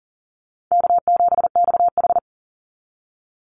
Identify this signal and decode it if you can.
Morse code, heard as a keyed tone, about 40 wpm, 700 Hz. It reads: K7XH
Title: Easy Call Signs - 40wpm